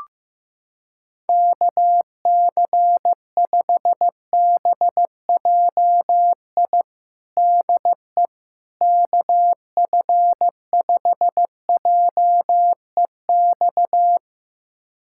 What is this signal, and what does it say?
Morse code, heard as a keyed tone, about 15 wpm, 700 Hz. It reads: KC5BJI DE KF5JEX
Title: USA Callsign DE Callsign - 6 Characters - 15wpm